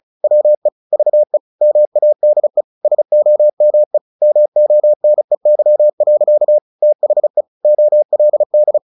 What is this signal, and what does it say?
Morse code, heard as a keyed tone, about 36 wpm, 600 Hz. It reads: WE VE MADE SOME MONEY. THE OLD
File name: the old man and the sea / the old man and the sea 1 35wpm